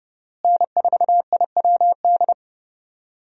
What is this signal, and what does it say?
Morse code, heard as a keyed tone, about 30 wpm, 700 Hz. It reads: N4IWD
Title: USA Call Signs - 5 Characters - Repeated 3x - 30wpm